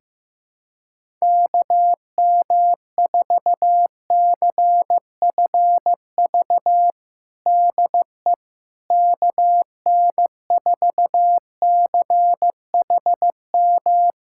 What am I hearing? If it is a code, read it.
Morse code, heard as a keyed tone, about 15 wpm, 700 Hz. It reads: KM4CFV DE KN4CHM
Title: USA Callsign DE Callsign - 6 Characters - 15wpm